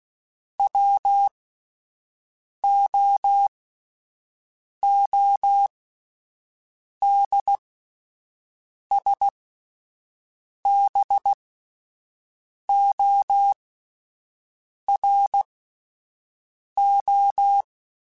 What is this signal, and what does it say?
Morse code, heard as a keyed tone, about 16 wpm, spaced as for 5 wpm, 800 Hz. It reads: WOODSBORO